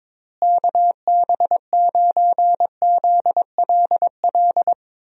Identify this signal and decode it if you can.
Morse code, heard as a keyed tone, about 22 wpm, 700 Hz. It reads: KB9ZLL
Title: USA Call Signs - 6 Characters - Repeated 3x - 22wpm